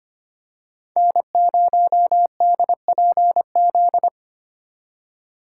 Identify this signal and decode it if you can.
Morse code, heard as a keyed tone, about 29 wpm, 700 Hz. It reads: N0DPZ
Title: Speed Racing - USA 1x3 Call Signs 25wpm